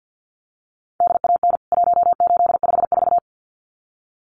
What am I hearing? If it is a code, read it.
Morse code, heard as a keyed tone, about 50 wpm, 700 Hz. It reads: DAN 1854
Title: CWT Exchange 50wpm